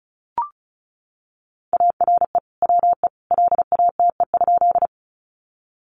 Morse code, heard as a keyed tone, about 34 wpm, 700 Hz. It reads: ARE WE LATE?
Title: Sentences from Top 300 Words 35wpm